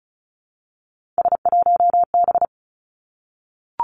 Morse code, heard as a keyed tone, about 36 wpm, 700 Hz. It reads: S1B
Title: Three Challenging Letter-Number Combinations 35wpm